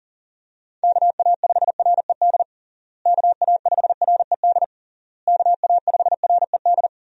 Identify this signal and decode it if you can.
Morse code, heard as a keyed tone, about 40 wpm, 700 Hz. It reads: KA5RED KA5RED KA5RED
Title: USA Call Signs - 6 Characters - Repeated 3x - 40wpm